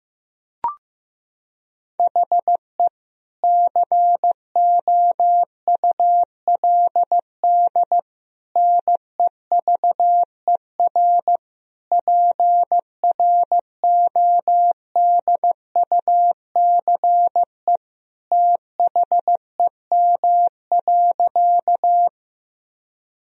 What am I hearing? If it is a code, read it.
Morse code, heard as a keyed tone, about 14 wpm, 700 Hz. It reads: HE COULD NEVER PRODUCE THEM.
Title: Sentences from Top 1000 Words - Review Part 3 - 15wpm